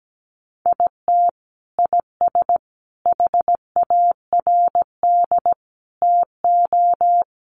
Morse code, heard as a keyed tone, about 17 wpm, 700 Hz. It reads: IT IS HARD TO
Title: Sets of 5 Words Encore2 17wpm